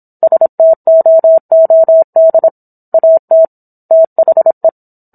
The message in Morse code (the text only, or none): STOOD AT THE